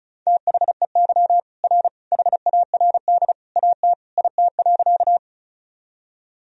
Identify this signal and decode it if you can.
Morse code, heard as a keyed tone, about 34 wpm, 700 Hz. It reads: THEY R HARD AT IT.
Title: Sentences from Top 300 Words and Abbreviations 35wpm